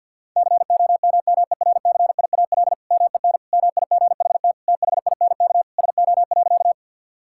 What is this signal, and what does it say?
Morse code, heard as a keyed tone, about 50 wpm, 700 Hz. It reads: COMMERCIAL MEN MIGHT THINK SO.
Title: Sentences from Top 1000 Words - Review Part 5 - 50wpm